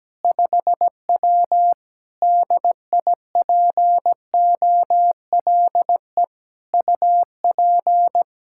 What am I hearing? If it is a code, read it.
Morse code, heard as a keyed tone, about 17 wpm, 700 Hz. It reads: 5W DIPOLE UP
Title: Practice QSO - Exchange 2 - Info - 17wpm